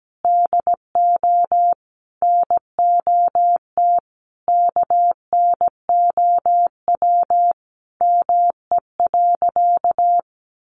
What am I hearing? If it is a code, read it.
Morse code, heard as a keyed tone, about 17 wpm, 700 Hz. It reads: DO NOT KNOW ME.
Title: Sentences from Top 600 Words 17wpm